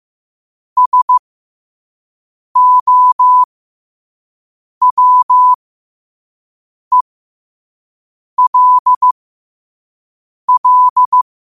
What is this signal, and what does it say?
Morse code, heard as a keyed tone, about 15 wpm, spaced as for 5 wpm, 1000 Hz. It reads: SOWELL
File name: qotdfiles / 5wpm